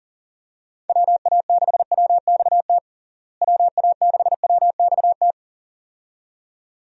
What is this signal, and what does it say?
Morse code, heard as a keyed tone, about 40 wpm, 700 Hz. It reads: WA6WXT WA6WXT
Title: USA Call Signs - 6 Characters - Repeated 2x - 40wpm